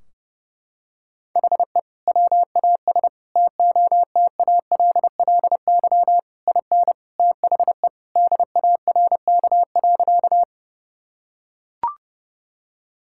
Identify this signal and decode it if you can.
Morse code, heard as a keyed tone, about 30 wpm, 700 Hz. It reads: HE WAS TOTALLY IN THE DARK.
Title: Sentences from Top 1600 Words 30wpm